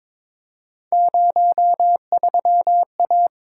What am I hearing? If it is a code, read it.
Morse code, heard as a keyed tone, about 21 wpm, 700 Hz. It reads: 03A
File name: practice / 22wpm / Three Letter-Number Combinations - 22wpm